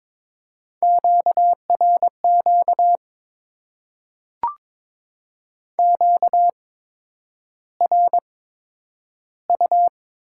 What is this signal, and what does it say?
Morse code, heard as a keyed tone, about 22 wpm, spaced as for 9 wpm, 700 Hz. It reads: QRQ QRU
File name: Top 90 QSO Elements - 8x Character Spacing - 22wpm